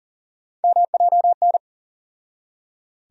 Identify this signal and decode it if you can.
Morse code, heard as a keyed tone, about 40 wpm, 700 Hz. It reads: MJN
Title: USA Call Sign Postfixes - 3 Letters 40wpm